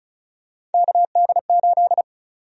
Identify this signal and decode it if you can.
Morse code, heard as a keyed tone, about 34 wpm, 700 Hz. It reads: KD8
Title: USA Call Sign Prefixes - 35wpm